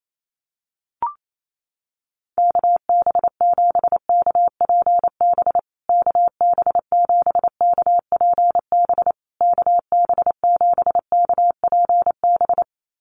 Morse code, heard as a keyed tone, about 28 wpm, 700 Hz. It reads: KB7KPB KB7KPB KB7KPB
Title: USA Call Signs Repeated 3x 28wpm